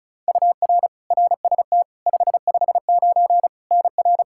Morse code, heard as a keyed tone, about 34 wpm, 700 Hz. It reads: UR RST 559 NR
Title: K1USN SST QSO Practice 35wpm